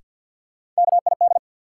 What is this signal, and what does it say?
Morse code, heard as a keyed tone, about 52 wpm, 700 Hz. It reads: KID
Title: Sentences from Top 1600 Words - New Word Drill Alt - 50wpm